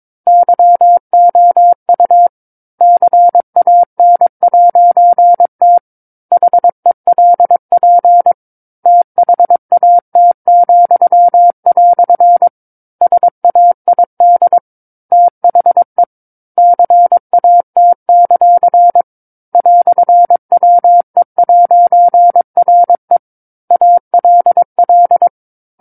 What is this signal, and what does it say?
Morse code, heard as a keyed tone, about 22 wpm, 700 Hz. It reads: YOU CAN'T HELP THAT," SAID THE CAT; "WE'RE ALL
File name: 06-Chapter-06 22 WPM